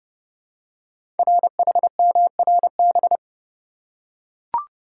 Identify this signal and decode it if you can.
Morse code, heard as a keyed tone, about 28 wpm, 700 Hz. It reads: RHMRB
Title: Code Groups - 5 Letters - 30wpm